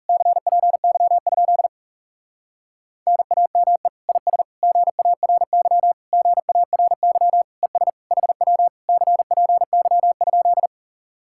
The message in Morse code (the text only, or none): CPY? NAME IS GARY GARY ES HW CPY?